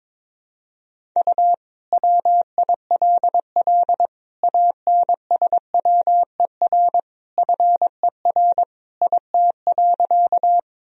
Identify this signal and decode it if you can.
Morse code, heard as a keyed tone, about 22 wpm, 700 Hz. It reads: U WILL ANSWER FER IT.
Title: Sentences from Top 300 Words and Abbreviations 22wpm